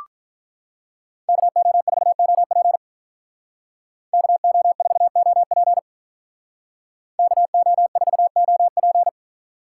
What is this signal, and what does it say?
Morse code, heard as a keyed tone, about 43 wpm, 700 Hz. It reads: KO4OP KO4OP KO4OP
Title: Speed Racing - USA 2x2 Call Signs 35wpm